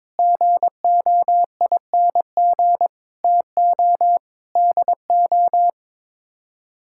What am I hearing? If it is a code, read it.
Morse code, heard as a keyed tone, about 22 wpm, 700 Hz. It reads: GOING TO DO
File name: practice / 22wpm / Sets of 6 Words - 22wpm